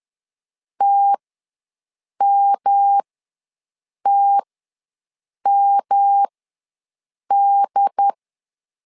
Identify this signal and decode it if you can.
Morse code, heard as a keyed tone, about 10 wpm, spaced as for 6 wpm, 800 Hz. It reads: TMTMD
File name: LEZIONI / 3 MWRO / 3 M ws 10wpm 2121